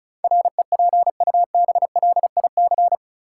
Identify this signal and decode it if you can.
Morse code, heard as a keyed tone, about 36 wpm, 700 Hz. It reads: REPUBLIC
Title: Sentences from Top 1500 Words - New Word Drill - 35wpm